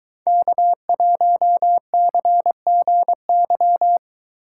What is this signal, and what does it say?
Morse code, heard as a keyed tone, about 23 wpm, 700 Hz. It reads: K1CGY